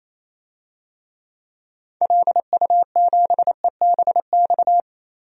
Morse code, heard as a keyed tone, about 27 wpm, 700 Hz. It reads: LU7EBX
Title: DXCC-100 Call Signs - 6 Characters - 28wpm